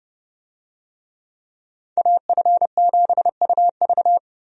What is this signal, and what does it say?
Morse code, heard as a keyed tone, about 30 wpm, 700 Hz. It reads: AF7UV
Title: USA Call Signs - 5 Characters - Repeated 2x - 30wpm